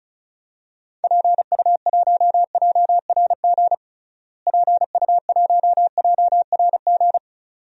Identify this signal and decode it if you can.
Morse code, heard as a keyed tone, about 34 wpm, 700 Hz. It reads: PU1JRG PU1JRG
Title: DXCC-100 Call Signs - 6 Characters - Repeated 2x - 35wpm